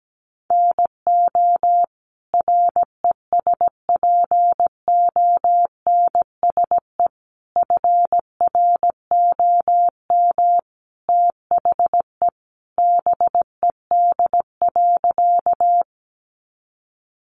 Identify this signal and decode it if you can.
Morse code, heard as a keyed tone, about 17 wpm, 700 Hz. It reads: NO RESPONSE FROM THE BED.
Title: Sentences from Top 1400 Words 17wpm